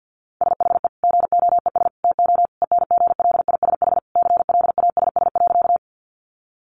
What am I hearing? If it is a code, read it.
Morse code, heard as a keyed tone, about 50 wpm, 700 Hz. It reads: SHE GOES TO ENGLISH CLASS.
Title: Additional Easy Phrases 50wpm